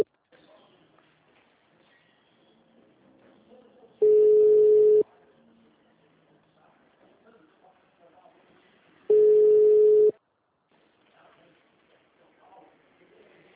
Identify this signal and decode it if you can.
Ringback tone (Europe, 425 Hz), with speech.